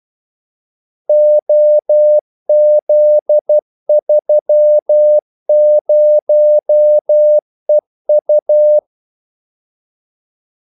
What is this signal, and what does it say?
Morse code, heard as a keyed tone, about 12 wpm, 600 Hz. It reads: OZ30EU
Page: 2 ekstra ordmellemrum mellem kaldesignalerne